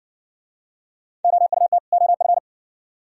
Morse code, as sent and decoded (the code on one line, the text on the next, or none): --- ..- - / --- ..-.
OUT OF